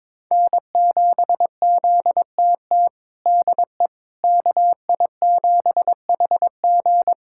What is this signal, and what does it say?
Morse code, heard as a keyed tone, about 22 wpm, 700 Hz. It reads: N7ZTT DE KI7HG
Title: USA Callsign DE Callsign - 5 Characters - 22wpm